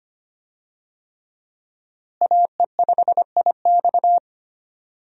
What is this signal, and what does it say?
Morse code, heard as a keyed tone, about 25 wpm, 700 Hz. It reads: AE5IX
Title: USA Call Signs - 5 Characters - Repeated 3x - 25wpm